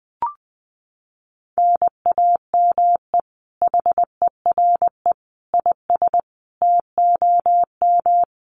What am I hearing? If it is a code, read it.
Morse code, heard as a keyed tone, about 19 wpm, 700 Hz. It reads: NAME HERE IS TOM
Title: Common QSO Phrases 20wpm